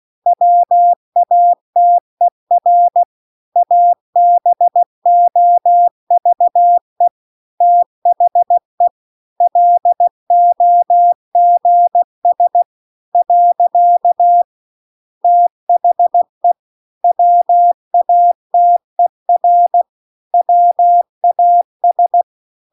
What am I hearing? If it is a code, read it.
Morse code, heard as a keyed tone, about 16 wpm, 700 Hz. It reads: WATER ABOVE THE LOGS . THE WATER WAS
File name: CW-The-Big-Twoheaded-River-2-16WPM